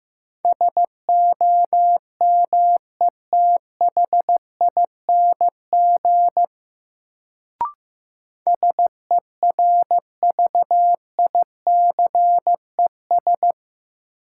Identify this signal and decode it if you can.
Morse code, heard as a keyed tone, about 15 wpm, 700 Hz. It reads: SOMETHING SERVICES
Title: Sets of 5 Words - New Word Drill - 15wpm